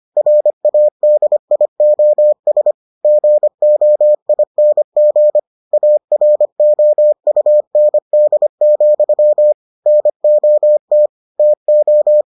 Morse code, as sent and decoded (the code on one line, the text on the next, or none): .-. .- -.. .. --- ... / --. --- .. -. --. / .- .-. --- ..- -. -.. --..-- / -. --- - / - ---
RADIOS GOING AROUND, NOT TO